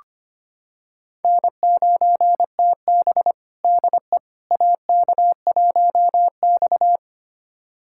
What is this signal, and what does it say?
Morse code, heard as a keyed tone, about 25 wpm, 700 Hz. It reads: N9TB DE AK1X
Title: USA Callsign DE Callsign - 4 Characters - 25wpm